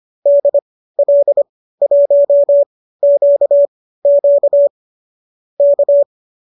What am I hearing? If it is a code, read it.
Morse code, heard as a keyed tone, about 25 wpm, spaced as for 15 wpm, 570 Hz. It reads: DL1QQ K